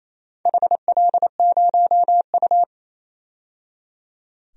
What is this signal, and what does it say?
Morse code, heard as a keyed tone, about 28 wpm, 700 Hz. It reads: HL0U
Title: DXCC-100 Call Signs - 4 Characters - 28wpm